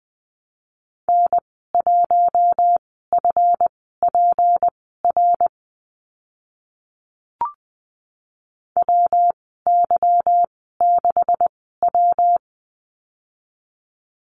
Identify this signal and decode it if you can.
Morse code, heard as a keyed tone, about 20 wpm, spaced as for 15 wpm, 700 Hz. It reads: N1FPR WY6W
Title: 167 Lesson 27g - Number 7 - Call Signs 20wpm